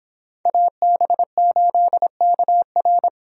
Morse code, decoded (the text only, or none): AB8KR